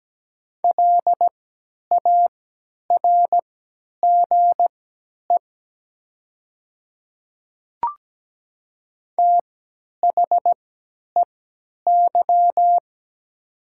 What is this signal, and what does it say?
Morse code, heard as a keyed tone, about 17 wpm, spaced as for 10 wpm, 700 Hz. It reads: LARGE THEY
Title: 3x Character Spacing - Top 200 Words - 17wpm